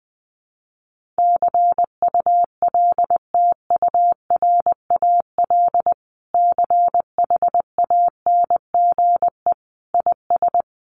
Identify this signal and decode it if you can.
Morse code, heard as a keyed tone, about 20 wpm, 700 Hz. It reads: CULTURAL CHANGE IS
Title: Sentences from Top 1500 Words 20wpm